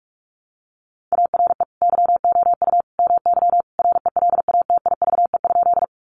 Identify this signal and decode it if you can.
Morse code, heard as a keyed tone, about 44 wpm, 700 Hz. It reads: ARE YOU MY RELATIVE?
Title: Sentences from Top 1400 Words 45wpm